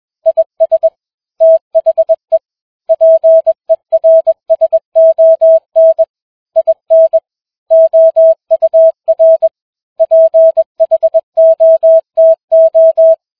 Morse code, heard as a keyed tone, about 21 wpm, 650 Hz. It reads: IS THE PERSON IN OUR PHOTO